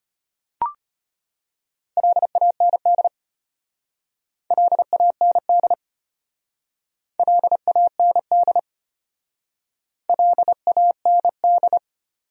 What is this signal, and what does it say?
Morse code, heard as a keyed tone, about 32 wpm, 700 Hz. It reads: LAND LAND LAND LAND